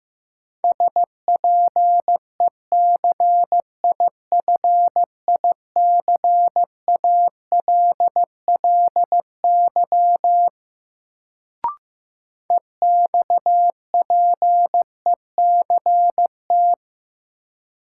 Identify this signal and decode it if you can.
Morse code, heard as a keyed tone, about 15 wpm, 700 Hz. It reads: SPECIFICALLY EXPECT
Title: Sentences from Top 1100 Words - New Word Drill - 15wpm